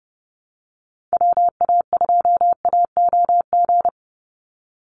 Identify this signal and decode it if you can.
Morse code, heard as a keyed tone, about 30 wpm, 700 Hz. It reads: WA2AOG
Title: Call Signs Repeated 2x - 30wpm